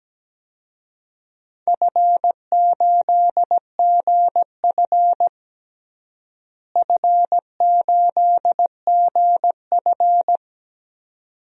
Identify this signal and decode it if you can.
Morse code, heard as a keyed tone, about 17 wpm, 700 Hz. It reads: F8GF F8GF
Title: DXCC-100 Call Signs - 4 Characters - Repeated 2x - 17wpm